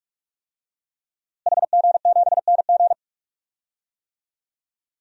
Morse code, heard as a keyed tone, about 45 wpm, 700 Hz. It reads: HG7NG
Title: DXCC-100 Call Signs - 5 Characters - Repeated 2x - 45wpm